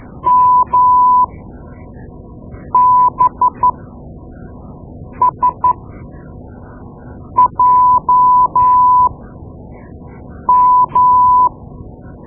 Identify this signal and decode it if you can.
Morse code, heard as a keyed tone, about 13 wpm, spaced as for 5 wpm, 1000 Hz. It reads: MBSJM
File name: Morse Lessons / Morse Lessons / Tape-2B